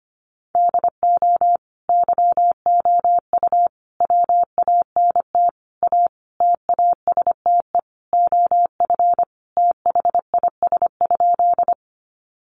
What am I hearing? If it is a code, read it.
Morse code, heard as a keyed tone, about 25 wpm, 700 Hz. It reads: DO YOU WANT A TASTE OF THIS?
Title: Sentences from Top 1100 Words 25wpm